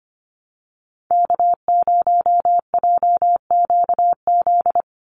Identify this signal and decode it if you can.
Morse code, heard as a keyed tone, about 25 wpm, 700 Hz. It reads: K0JQZ
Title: NA SOTA Activator Callsigns 25wpm